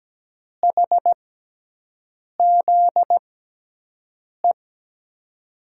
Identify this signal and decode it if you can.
Morse code, heard as a keyed tone, about 17 wpm, spaced as for 6 wpm, 700 Hz. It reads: HZE
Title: USA Call Signs - 6 Characters - 6x Character Spacing - 17wpm